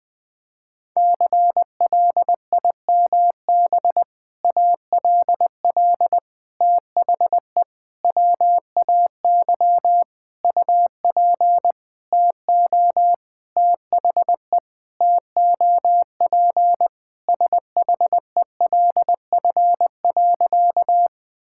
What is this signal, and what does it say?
Morse code, heard as a keyed tone, about 20 wpm, 700 Hz. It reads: CLIMB ALL THE WAY UP TO THE TOP SHELF.
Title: Sentences with 6-Word Phrases - 20wpm